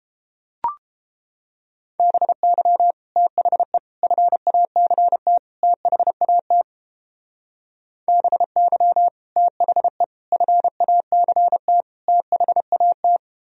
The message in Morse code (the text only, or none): BY THE FACT THAT BY THE FACT THAT